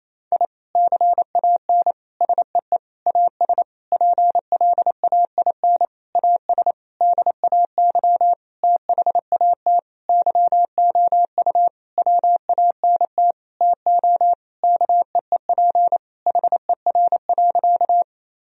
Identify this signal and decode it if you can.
Morse code, heard as a keyed tone, about 28 wpm, 700 Hz. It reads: I CAN SEE AS PLAIN AS DAY THAT YOU WANT TO KEEP HER.
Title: Sentences from Top 1000 Words - Review Part 2 - 28wpm